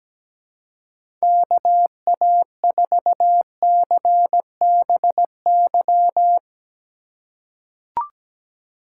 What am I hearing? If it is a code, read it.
Morse code, heard as a keyed tone, about 17 wpm, 700 Hz. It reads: KA4CBY
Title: USA Call Signs - 6 Characters - 2x Character Spacing - 17wpm